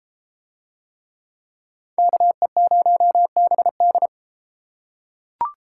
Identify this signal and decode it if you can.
Morse code, heard as a keyed tone, about 32 wpm, 700 Hz. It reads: KE0BD